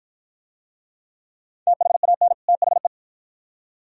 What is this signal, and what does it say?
Morse code, heard as a keyed tone, about 54 wpm, 700 Hz. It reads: THAN THE